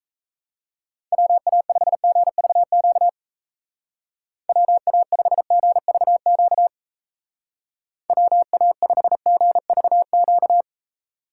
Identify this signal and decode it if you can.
Morse code, heard as a keyed tone, about 33 wpm, 700 Hz. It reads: WA5GVQ WA5GVQ WA5GVQ
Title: Speed Racing - USA 2x3 Call Signs 28wpm